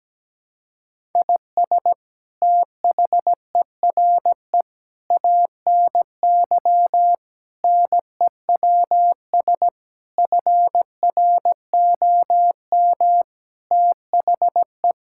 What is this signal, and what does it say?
Morse code, heard as a keyed tone, about 17 wpm, 700 Hz. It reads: IS THERE ANY NEWS FROM THE
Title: Sentences from Top 1000 Words - Review Part 5 - 17wpm